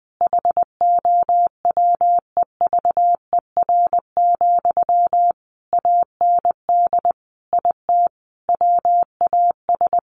Morse code, heard as a keyed tone, about 20 wpm, 700 Hz. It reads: HOWEVER, AND IT WAS
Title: Sentences from Top 600 Words 20wpm